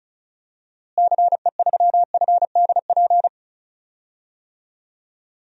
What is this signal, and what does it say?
Morse code, heard as a keyed tone, about 34 wpm, 700 Hz. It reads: CE3FDP
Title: DXCC-100 Call Signs - 6 Characters - Repeated 2x - 35wpm